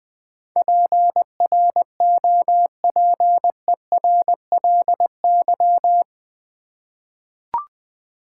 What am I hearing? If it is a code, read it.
Morse code, heard as a keyed tone, about 21 wpm, 700 Hz. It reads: PROPERLY
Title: 8 Letter Words - First 3 Letters Given - 20wpm